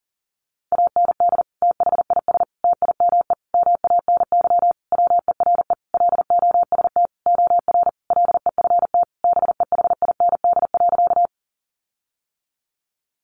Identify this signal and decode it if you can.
Morse code, heard as a keyed tone, about 40 wpm, 700 Hz. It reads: AND THIS TIME MANY WERE LOST OR LEFT BEHIND.
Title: Sentences from Top 1400 Words 40wpm